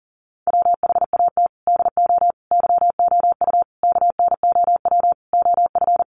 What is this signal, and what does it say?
Morse code, heard as a keyed tone, about 40 wpm, 700 Hz. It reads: WHAT DO YOU KNOW OF
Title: Sentences from Top 100 Words 40wpm